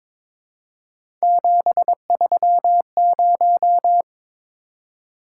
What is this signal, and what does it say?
Morse code, heard as a keyed tone, about 22 wpm, 700 Hz. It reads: 730
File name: practice / 22wpm / Three Letter-Number Combinations - 22wpm